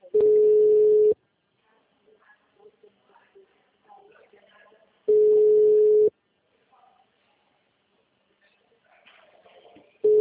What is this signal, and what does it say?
Ringback tone (Europe, 425 Hz), with speech.
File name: out- -2019- - - .3194